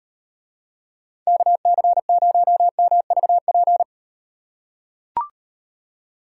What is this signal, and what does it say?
Morse code, heard as a keyed tone, about 38 wpm, 700 Hz. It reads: KC0MVP